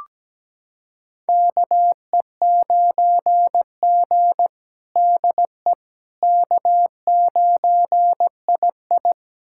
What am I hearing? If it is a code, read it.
Morse code, heard as a keyed tone, about 17 wpm, 700 Hz. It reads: KE9G DE K9II
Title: USA Callsign DE Callsign - 4 Characters - 17wpm